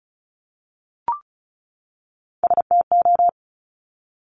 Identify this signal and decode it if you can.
Morse code, heard as a keyed tone, about 36 wpm, 700 Hz. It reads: STO
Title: Top Three Letter Combinations in Words 35wpm